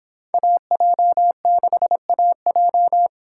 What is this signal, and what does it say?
Morse code, heard as a keyed tone, about 24 wpm, 700 Hz. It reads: AJ6AJ